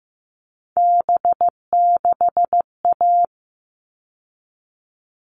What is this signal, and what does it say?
Morse code, heard as a keyed tone, about 15 wpm, 700 Hz. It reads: B6A
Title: All Cut Numbers - 3-Digit 15wpm